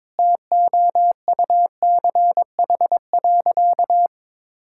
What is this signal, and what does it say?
Morse code, heard as a keyed tone, about 22 wpm, 700 Hz. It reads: TOUCH.
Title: Sentences from Top 1600 Words 22wpm